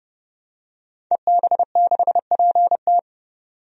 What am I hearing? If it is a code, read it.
Morse code, heard as a keyed tone, about 30 wpm, 700 Hz. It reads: EB6PT
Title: DXCC-100 Call Signs - 5 Characters - 30wpm